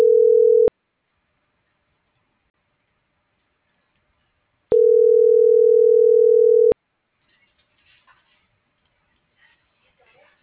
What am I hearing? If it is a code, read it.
Ringback tone (North America), 440 + 480 Hz.
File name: CENTRAL SOTOVA / out alejandro.rodriguez - .51606